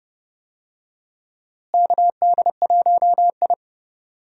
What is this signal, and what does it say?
Morse code, heard as a keyed tone, about 30 wpm, 700 Hz. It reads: KD1I
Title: USA Call Signs - 4 Characters - Repeated 3x - 30wpm